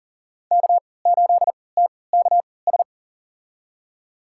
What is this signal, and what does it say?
Morse code, heard as a keyed tone, about 40 wpm, spaced as for 23 wpm, 700 Hz. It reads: K8TKS
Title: USA Call Signs - 5 Characters - 3x Character Spacing - 40wpm